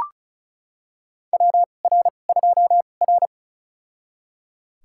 Morse code, heard as a keyed tone, about 34 wpm, spaced as for 25 wpm, 700 Hz. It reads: WR2R
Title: USA Call Signs - 4 Characters - 2x Character Spacing - 35wpm